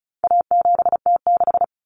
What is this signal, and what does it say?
Morse code, heard as a keyed tone, about 34 wpm, 700 Hz. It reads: A7T6
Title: Common Cut Numbers - 4-Digit 35wpm